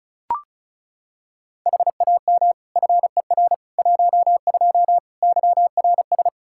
Morse code, heard as a keyed tone, about 36 wpm, 700 Hz. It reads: HAM FER 12 YRS
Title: Practice QSO - Exchange 3 - More Info - 35wpm